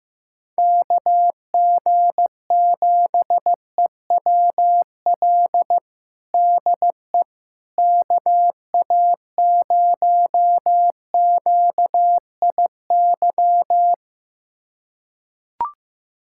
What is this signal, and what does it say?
Morse code, heard as a keyed tone, about 15 wpm, 700 Hz. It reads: KG7EWL DE KA0QIY
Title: USA Callsign DE Callsign - 6 Characters - 15wpm